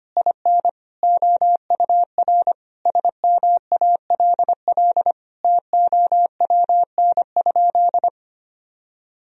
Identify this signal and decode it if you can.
Morse code, heard as a keyed tone, about 25 wpm, 700 Hz. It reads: IN OUR SMALL TOWN?
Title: Sentences from Top 1000 Words - Review Part 4 - 25wpm